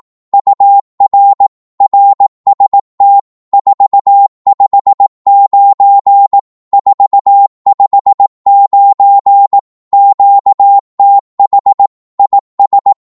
Morse code, heard as a keyed tone, about 18 wpm, 800 Hz. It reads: UR RST 459 459 QTH IS
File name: Morse / qso-18wpm